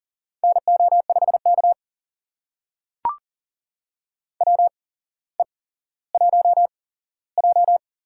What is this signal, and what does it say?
Morse code, heard as a keyed tone, about 38 wpm, spaced as for 17 wpm, 700 Hz. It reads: NO5K WE1J
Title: USA Call Signs - 4 Characters - 8x Character Spacing - 40wpm